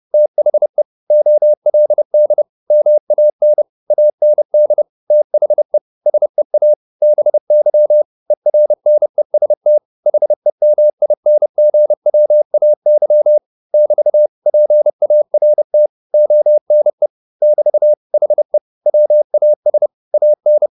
Morse code, heard as a keyed tone, about 30 wpm, 600 Hz. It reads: THE OLD MAN AND THE SEA BY ERNEST HEMINGWAY <BT> PART ONE <BT> HE WAS AN
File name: the old man and the sea / the old man and the sea 1 30wpm